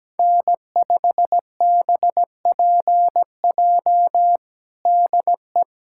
Morse code, heard as a keyed tone, about 17 wpm, 700 Hz. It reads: N5BPJ DE
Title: K1USN SST QSO Practice 17wpm